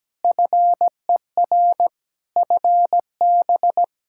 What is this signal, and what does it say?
Morse code, heard as a keyed tone, about 17 wpm, 700 Hz. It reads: FER FB
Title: K1USN SST QSO Practice 17wpm